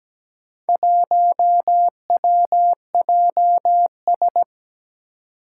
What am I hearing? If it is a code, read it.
Morse code, heard as a keyed tone, about 17 wpm, 700 Hz. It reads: 1WJS
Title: Four Challenging Letter-Number Combinations 17wpm